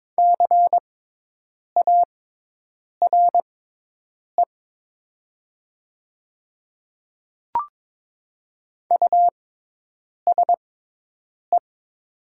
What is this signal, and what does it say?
Morse code, heard as a keyed tone, about 22 wpm, spaced as for 8 wpm, 700 Hz. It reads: CARE USE
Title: 6x Character Spacing - Top 500 Words - 22wpm